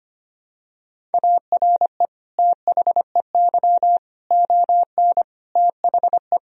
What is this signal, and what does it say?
Morse code, heard as a keyed tone, about 25 wpm, 700 Hz. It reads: ARE THEY ON THE
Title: Sentences from Top 1000 Words - Review Part 2 - 25wpm